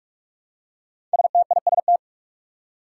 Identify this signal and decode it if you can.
Morse code, heard as a keyed tone, about 44 wpm, 700 Hz. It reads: STIST
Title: Five Challenging Letter-Number Combinations 45wpm